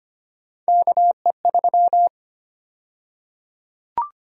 Morse code, heard as a keyed tone, about 25 wpm, 700 Hz. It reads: KE3
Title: USA Call Sign Prefixes - 25wpm